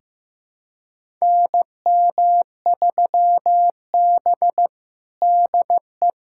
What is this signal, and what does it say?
Morse code, heard as a keyed tone, about 14 wpm, 700 Hz. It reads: NM3B DE
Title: USA Callsign DE Callsign - 4 Characters - 15wpm